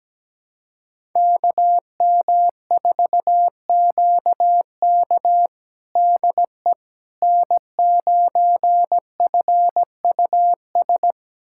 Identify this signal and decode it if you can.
Morse code, heard as a keyed tone, about 17 wpm, 700 Hz. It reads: KM4QK DE N9FUS
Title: USA Callsign DE Callsign - 5 Characters - 17wpm